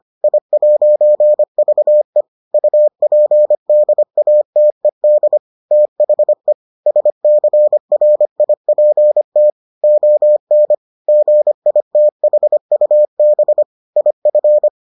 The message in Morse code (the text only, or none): I'VE UPDATED THE SCRIPT ON GITHUB IF